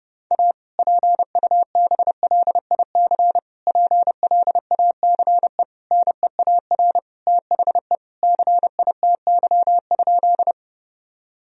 Morse code, heard as a keyed tone, about 32 wpm, 700 Hz. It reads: A PUBLIC PLACE NEAR THE CITY?
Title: Sentences from Top 1000 Words - Review Part 3 - 30wpm